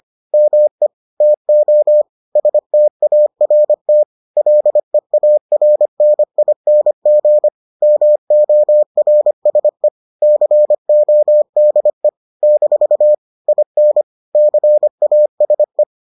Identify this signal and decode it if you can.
Morse code, heard as a keyed tone, about 25 wpm, 600 Hz. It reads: ME TO START LEARNING MORSE CODE - IN CASE